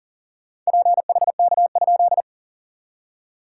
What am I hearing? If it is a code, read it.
Morse code, heard as a keyed tone, about 40 wpm, 690 Hz. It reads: PHK?
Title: SOTA QSO Practice 40wpm